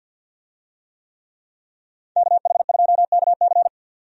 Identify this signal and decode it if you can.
Morse code, heard as a keyed tone, about 46 wpm, 700 Hz. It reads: KH2KC
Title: USA Call Signs - 5 Characters - 50wpm